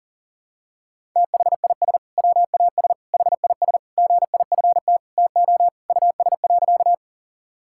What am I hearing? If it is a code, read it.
Morse code, heard as a keyed tone, about 40 wpm, 700 Hz. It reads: THIS WAS HIS GIFT TO US.
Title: Sentences from Top 1000 Words - Review Part 6 - 40wpm